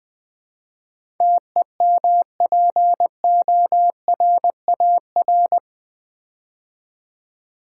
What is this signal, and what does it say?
Morse code, heard as a keyed tone, about 20 wpm, 700 Hz. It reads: TEMPORAR
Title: Build A Word - 4 Letters At A Time - 11-Letter Words - 20wpm